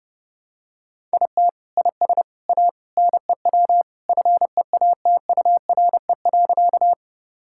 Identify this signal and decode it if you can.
Morse code, heard as a keyed tone, about 30 wpm, 700 Hz. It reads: IT IS A NEW FEATURE.
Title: Sentences from Top 1000 Words - Review Part 5 - 30wpm